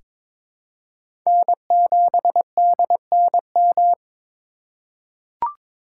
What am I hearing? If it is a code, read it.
Morse code, heard as a keyed tone, about 22 wpm, 700 Hz. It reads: N7DNM
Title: USA Call Signs - 5 Characters - 3x Character Spacing - 22wpm